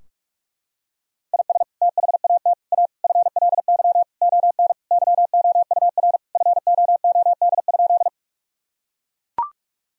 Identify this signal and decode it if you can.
Morse code, heard as a keyed tone, about 42 wpm, 700 Hz. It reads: IS THAT A FLY ON YOUR FOOD?
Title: Sentences from Top 1000 Words - Review Part 2 - 45wpm